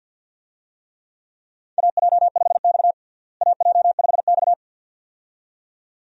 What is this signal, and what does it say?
Morse code, heard as a keyed tone, about 48 wpm, 700 Hz. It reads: AJ5X AJ5X
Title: USA Call Signs - 4 Characters - Repeated 2x - 50wpm